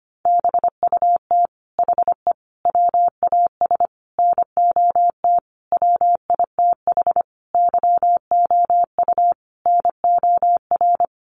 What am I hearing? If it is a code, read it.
Morse code, heard as a keyed tone, about 25 wpm, 700 Hz. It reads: BUT HE WAS NOT WITH YOU NOR
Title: Sentences from Top 1500 Words 25wpm